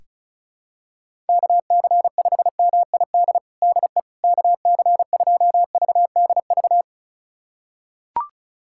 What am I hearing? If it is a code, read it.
Morse code, heard as a keyed tone, about 34 wpm, 700 Hz. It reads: KC5MID DE KC2VDV
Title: USA Callsign DE Callsign - 6 Characters - 35wpm